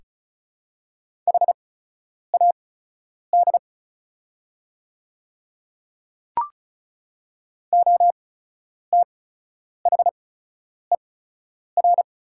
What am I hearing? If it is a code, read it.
Morse code, heard as a keyed tone, about 36 wpm, spaced as for 10 wpm, 700 Hz. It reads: HAD OTHER
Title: 8x Character Spacing - Top 200 Words - 35wpm